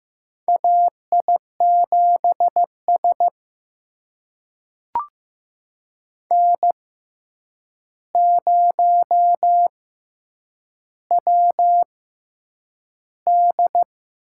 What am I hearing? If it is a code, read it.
Morse code, heard as a keyed tone, about 15 wpm, spaced as for 8 wpm, 700 Hz. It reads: AI7S N0WD
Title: USA Call Signs - 4 Characters - 6x Character Spacing - 15wpm